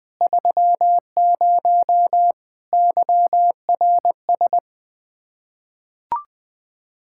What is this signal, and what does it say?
Morse code, heard as a keyed tone, about 20 wpm, 700 Hz. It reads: 30 YRS
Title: Practice QSO - Exchange 3 - More Info - 20wpm